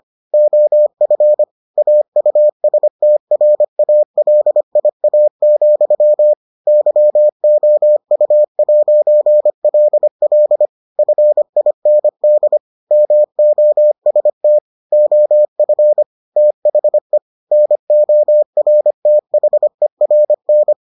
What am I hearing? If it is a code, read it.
Morse code, heard as a keyed tone, about 25 wpm, 600 Hz. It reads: OF AUSTRALIA, YOU'LL FIND MOST OF THE NORTHERN